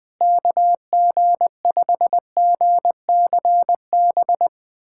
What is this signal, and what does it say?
Morse code, heard as a keyed tone, about 21 wpm, 700 Hz. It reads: KG5GCB
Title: USA Callsign DE Callsign 20wpm